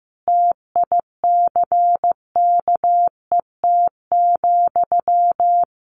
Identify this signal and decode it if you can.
Morse code, heard as a keyed tone, about 15 wpm, 700 Hz. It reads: TICKET,
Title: Sentences from Top 1400 Words 15wpm